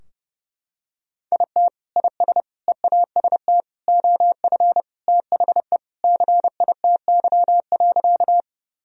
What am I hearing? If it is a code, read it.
Morse code, heard as a keyed tone, about 30 wpm, 700 Hz. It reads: IT IS EAST OF THE CITY.
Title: Sentences from Top 1000 Words - Review Part 3 - 30wpm